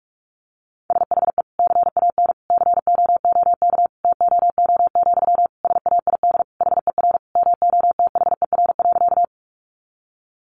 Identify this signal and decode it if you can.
Morse code, heard as a keyed tone, about 44 wpm, 700 Hz. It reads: SHE CAN COOK TOO, SAID HER MOTHER.
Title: Sentences from Top 700 Words 45wpm